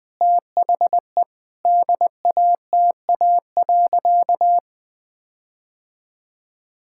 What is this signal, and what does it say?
Morse code, heard as a keyed tone, about 20 wpm, 700 Hz. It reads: THE DATA.
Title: Sentences with 6-Word Phrases - 20wpm